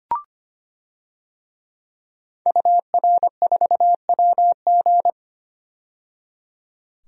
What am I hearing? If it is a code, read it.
Morse code, heard as a keyed tone, about 25 wpm, 700 Hz. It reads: UR4WG
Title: DXCC-100 Call Signs - 5 Characters - 25wpm